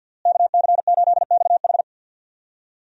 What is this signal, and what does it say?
Morse code, heard as a keyed tone, about 51 wpm, 700 Hz. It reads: KC8XH
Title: USA Call Signs - 5 Characters - Repeated 3x - 50wpm